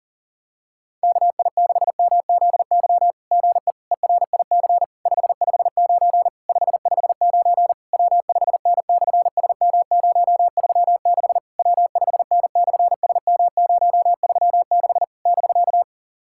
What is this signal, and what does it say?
Morse code, heard as a keyed tone, about 40 wpm, 700 Hz. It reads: KI6MZY GE ERIC 559 559 W5N/SM036 W5N/SM036 <BK>
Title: SOTA QSO Practice 40wpm